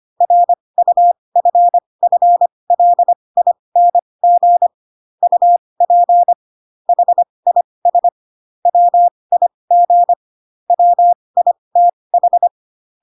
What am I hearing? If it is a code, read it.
Morse code, heard as a keyed tone, about 25 wpm, spaced as for 20 wpm, 700 Hz. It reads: RUFFLING UP HIS WIG WITH
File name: Pinocchio / Pinocchio ch1 15 25 ik1hge